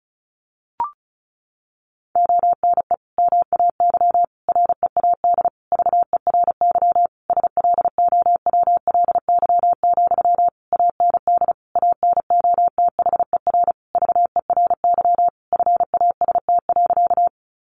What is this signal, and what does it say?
Morse code, heard as a keyed tone, about 34 wpm, 700 Hz. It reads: ONE MAY READ VERY SLOWLY, AND ANOTHER VERY FAST.
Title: Sentences from Top 1100 Words 35wpm